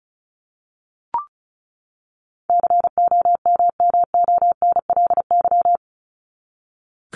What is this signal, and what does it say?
Morse code, heard as a keyed tone, about 34 wpm, 700 Hz. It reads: COMMONLY
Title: LY Suffix - Top 100 Words with Base Words 35wpm